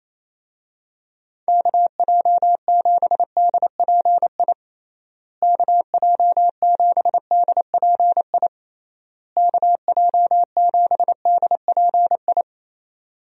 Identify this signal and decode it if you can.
Morse code, heard as a keyed tone, about 28 wpm, 700 Hz. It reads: KJ7DPI KJ7DPI KJ7DPI
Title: USA Call Signs - 6 Characters - Repeated 3x - 28wpm